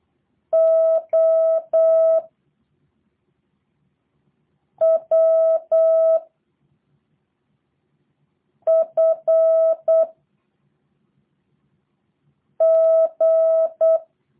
Morse code, heard as a keyed tone, about 8 wpm, 650 Hz. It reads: O W F G
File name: jh8cbh / 2023cw / review1-4